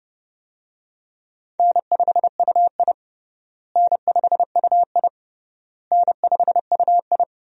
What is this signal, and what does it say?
Morse code, heard as a keyed tone, about 32 wpm, 700 Hz. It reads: N5UI N5UI N5UI
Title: USA Call Signs - 4 Characters - Repeated 3x - 30wpm